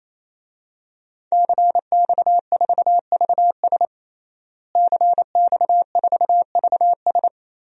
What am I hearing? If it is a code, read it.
Morse code, heard as a keyed tone, about 28 wpm, 700 Hz. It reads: CX4VS CX4VS
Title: DXCC-100 Call Signs - 5 Characters - Repeated 3x - 28wpm